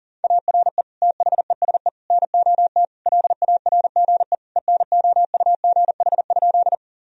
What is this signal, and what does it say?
Morse code, heard as a keyed tone, about 40 wpm, 700 Hz. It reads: ARE THESE NOT LARGE ENOUGH?
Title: Sentences from Top 1000 Words - Review Part 1 - 40wpm